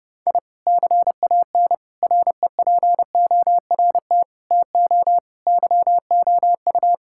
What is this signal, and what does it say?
Morse code, heard as a keyed tone, about 32 wpm, 700 Hz. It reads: I CAN REPORT TO YOU
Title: Sentences from Top 1000 Words - Review Part 6 - 30wpm